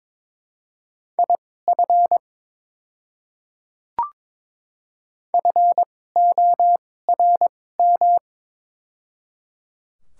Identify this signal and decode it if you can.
Morse code, heard as a keyed tone, about 22 wpm, spaced as for 16 wpm, 700 Hz. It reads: IF FORM
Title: 2x Character Spacing - Top 200 Words - 22wpm